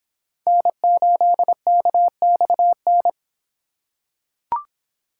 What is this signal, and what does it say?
Morse code, heard as a keyed tone, about 26 wpm, 700 Hz. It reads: N8KXN